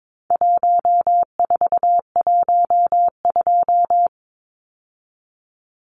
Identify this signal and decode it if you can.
Morse code, heard as a keyed tone, about 22 wpm, 700 Hz. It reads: 1412
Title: Numbers - 4-Digit 22wpm